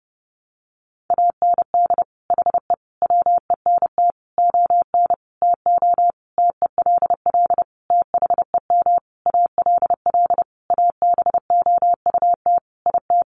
Morse code, heard as a keyed tone, about 30 wpm, 700 Hz. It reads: AND HE WENT ON TO TELL THEM ALL ABOUT IT
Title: Sentences from Top 900 Words 30wpm